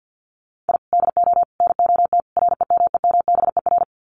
Morse code, heard as a keyed tone, about 50 wpm, 700 Hz. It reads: I DO NOT REMEMBER
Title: Sets of 4 Words 50wpm